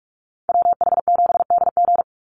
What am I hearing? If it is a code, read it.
Morse code, heard as a keyed tone, about 44 wpm, 700 Hz. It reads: WH7DG
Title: USA Call Signs 45wpm